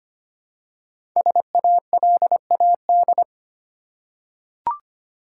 Morse code, heard as a keyed tone, about 25 wpm, 700 Hz. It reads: SALAD
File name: 5 Letter Words - First 3 Letters Given - 25wpm